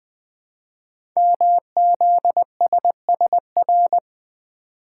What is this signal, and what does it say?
Morse code, heard as a keyed tone, about 20 wpm, 700 Hz. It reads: MZSSR
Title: Code Groups - 5 Letters - 20wpm